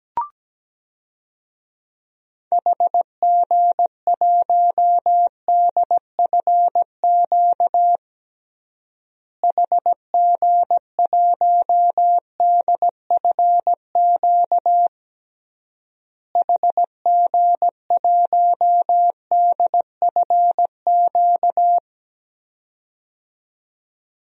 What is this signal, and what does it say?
Morse code, heard as a keyed tone, about 17 wpm, 700 Hz. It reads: HG1DFQ HG1DFQ HG1DFQ
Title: DXCC-100 Call Signs - 6 Characters - Repeated 3x - 17wpm